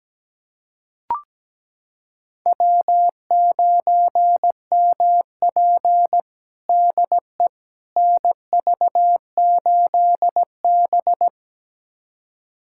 Morse code, heard as a keyed tone, about 17 wpm, 700 Hz. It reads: W9MP DE NV8B
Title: USA Callsign DE Callsign - 4 Characters - 17wpm